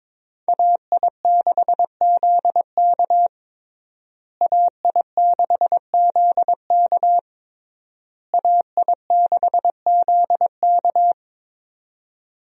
Morse code, heard as a keyed tone, about 22 wpm, 700 Hz. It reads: AI6ZK AI6ZK AI6ZK
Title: USA Call Signs - 5 Characters - Repeated 3x - 22wpm